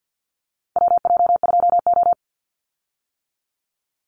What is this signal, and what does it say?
Morse code, heard as a keyed tone, about 50 wpm, 700 Hz. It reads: WJ2O
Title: Call Signs - 50wpm